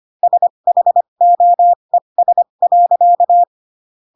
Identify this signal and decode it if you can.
Morse code, heard as a keyed tone, about 25 wpm, spaced as for 22 wpm, 700 Hz. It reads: SHOES.
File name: Pinocchio / Pinocchio ch3 18 25 ik1hge